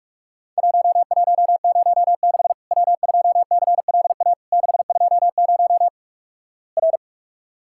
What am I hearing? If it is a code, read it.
Morse code, heard as a keyed tone, about 46 wpm, 700 Hz. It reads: 1106 W2CLA 620 R
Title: Worked All Europe DX Contest - QTC Exchange - 45wpm